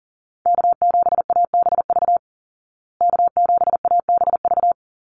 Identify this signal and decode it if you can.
Morse code, heard as a keyed tone, about 40 wpm, 700 Hz. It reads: K7ABV K7ABV
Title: Call Signs Repeated 3x - 40wpm